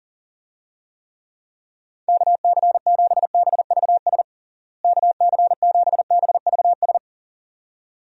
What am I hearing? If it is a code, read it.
Morse code, heard as a keyed tone, about 37 wpm, 700 Hz. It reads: KC7BVS KC7BVS
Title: USA Call Signs - 6 Characters - Repeated 2x - 40wpm